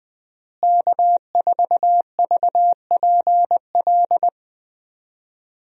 Morse code, heard as a keyed tone, about 21 wpm, 700 Hz. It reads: K4VPL
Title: USA Call Signs - 5 Characters - Repeated 3x - 20wpm